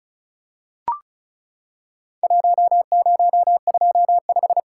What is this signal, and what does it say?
Morse code, heard as a keyed tone, about 36 wpm, 700 Hz. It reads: 1025
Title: Worked All Europe DX Contest - QTC Exchange - 35wpm